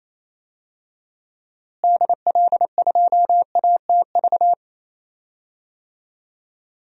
Morse code, heard as a keyed tone, about 28 wpm, 700 Hz. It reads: DL2ATV
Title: DXCC-100 Call Signs - 6 Characters - Repeated 3x - 28wpm